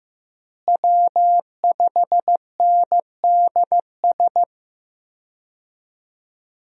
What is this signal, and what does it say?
Morse code, heard as a keyed tone, about 15 wpm, 700 Hz. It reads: W5NDS
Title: USA Call Signs - 5 Characters - Repeated 2x - 15wpm